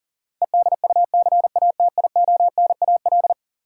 Morse code, heard as a keyed tone, about 40 wpm, 700 Hz. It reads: EDUCATIONAL
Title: Sentences from Top 1200 Words - New Word Drill Alt - 40wpm